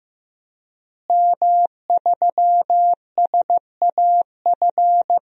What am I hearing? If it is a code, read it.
Morse code, heard as a keyed tone, about 15 wpm, 700 Hz. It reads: M3SAF
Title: DXCC-100 Call Signs - 5 Characters - Repeated 3x - 15wpm